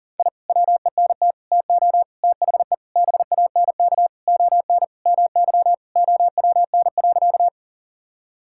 Morse code, heard as a keyed tone, about 40 wpm, 700 Hz. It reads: I WENT TO THE BANK ON MY OWN.
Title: Sentences from Top 1000 Words - Review Part 6 - 40wpm